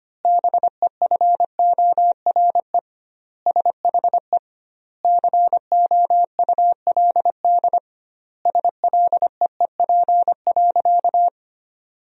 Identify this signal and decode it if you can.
Morse code, heard as a keyed tone, about 25 wpm, 700 Hz. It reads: BEFORE SHE COULD SLEEP.
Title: 2x Word-Spacing - Sentences from Top 500 Words - 25wpm